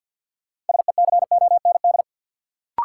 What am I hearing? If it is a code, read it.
Morse code, heard as a keyed tone, about 53 wpm, 700 Hz. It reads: SECOND
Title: Sets of 2 Words - New Word Drill - 50wpm